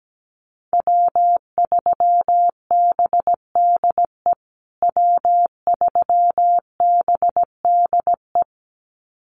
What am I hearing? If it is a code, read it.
Morse code, heard as a keyed tone, about 17 wpm, 700 Hz. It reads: W3BDE W3BDE
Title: USA Call Signs Repeated 2x 17wpm